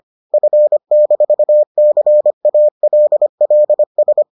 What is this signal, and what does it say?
Morse code, heard as a keyed tone, about 25 wpm, 600 Hz. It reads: F-CALLS